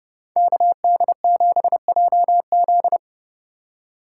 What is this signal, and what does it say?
Morse code, heard as a keyed tone, about 30 wpm, 700 Hz. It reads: KD7JZ